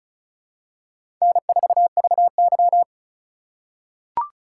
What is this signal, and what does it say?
Morse code, heard as a keyed tone, about 34 wpm, 700 Hz. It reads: N4VY
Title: USA Call Signs - 4 Characters - 4x Character Spacing - 35wpm